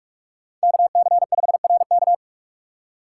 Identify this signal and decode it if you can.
Morse code, heard as a keyed tone, about 46 wpm, 700 Hz. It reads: KC5RK
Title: USA Call Signs - 5 Characters - 2x Character Spacing - 45wpm